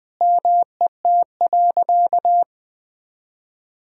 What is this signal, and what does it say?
Morse code, heard as a keyed tone, about 20 wpm, 700 Hz. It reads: MET.
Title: Sentences with 7-Word Phrases - Spoken Beginning - 20wpm